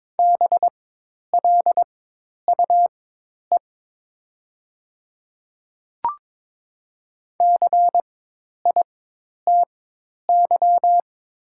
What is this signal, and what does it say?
Morse code, heard as a keyed tone, about 22 wpm, spaced as for 10 wpm, 700 Hz. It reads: BLUE CITY
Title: 4x Character Spacing - Top 500 Words - 22wpm